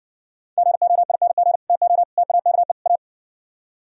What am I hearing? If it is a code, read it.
Morse code, heard as a keyed tone, about 69 wpm, 700 Hz. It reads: GOING TO TAKE A